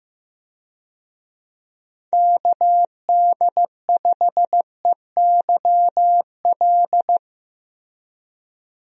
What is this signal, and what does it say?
Morse code, heard as a keyed tone, about 15 wpm, 700 Hz. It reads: KD5EYL
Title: USA Call Signs - 6 Characters - Repeated 3x - 15wpm